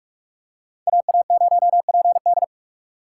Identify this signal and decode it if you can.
Morse code, heard as a keyed tone, about 45 wpm, 700 Hz. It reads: AA0PD
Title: USA Call Signs - 5 Characters - 2x Character Spacing - 45wpm